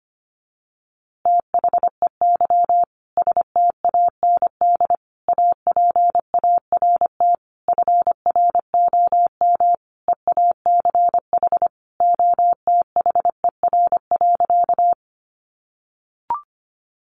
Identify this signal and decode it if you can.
Morse code, heard as a keyed tone, about 25 wpm, 700 Hz. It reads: THEY STAND APART FROM EACH OTHER.
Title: Easy Phrases 25wpm